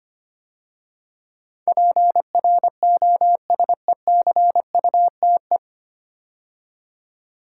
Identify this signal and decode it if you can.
Morse code, heard as a keyed tone, about 25 wpm, 700 Hz. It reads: PROSECUTE
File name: practice / 25wpm / 9 Letter Words - First 3 Letters Given - 25wpm